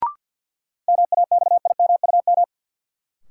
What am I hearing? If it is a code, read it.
Morse code, heard as a keyed tone, about 53 wpm, 700 Hz. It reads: MAXIMUM
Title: Sentences from Top 1300 Words - New Word Drill Alt - 50wpm